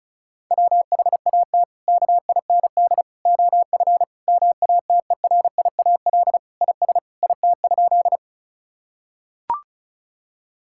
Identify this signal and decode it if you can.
Morse code, heard as a keyed tone, about 36 wpm, 700 Hz. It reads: WHAT KIND OF MATERIAL IS IT?
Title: Sentences from Top 1000 Words - Review Part 5 - 35wpm